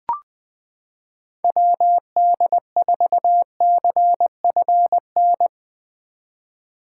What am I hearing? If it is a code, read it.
Morse code, heard as a keyed tone, about 20 wpm, 700 Hz. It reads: WD4CFN
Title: 2023 SOTA Chaser Call Signs - 20wpm